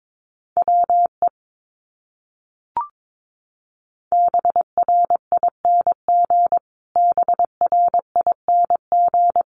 Morse code, heard as a keyed tone, about 22 wpm, 700 Hz. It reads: WE BRING BRING
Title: Top 500 Words Repeated 22wpm